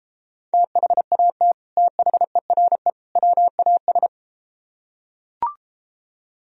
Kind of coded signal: Morse code

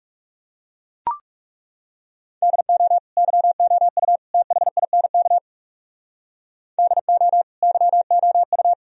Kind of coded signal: Morse code

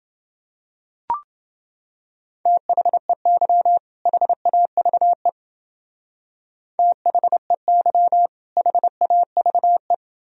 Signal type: Morse code

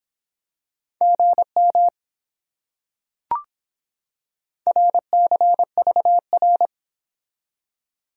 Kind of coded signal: Morse code

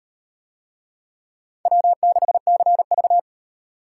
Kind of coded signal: Morse code

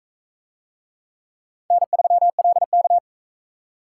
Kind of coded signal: Morse code

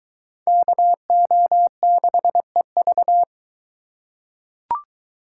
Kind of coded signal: Morse code

